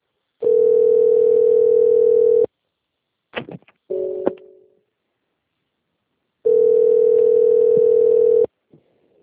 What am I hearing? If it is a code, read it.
Ringback tone (North America), 440 + 480 Hz.
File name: rg-1600-unknown- - - .65428